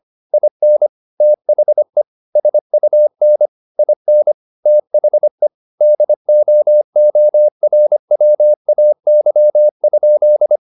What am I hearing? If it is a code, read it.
Morse code, heard as a keyed tone, about 25 wpm, 600 Hz. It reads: IN THE SUN IN THE DOORWAY?
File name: the old man and the sea part 1 25wpm